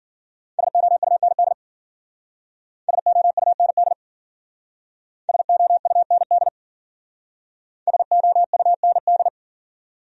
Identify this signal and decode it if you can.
Morse code, heard as a keyed tone, about 50 wpm, 700 Hz. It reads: SOUND SOUND SOUND SOUND
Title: Speed-Racing Top 100 Words 40wpm